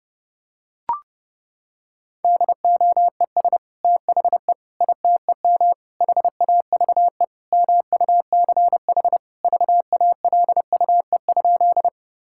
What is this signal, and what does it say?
Morse code, heard as a keyed tone, about 30 wpm, 700 Hz. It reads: DOES THE ITEM HAVE MUCH VALUE?
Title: Sentences from Top 1000 Words - Review Part 6 - 30wpm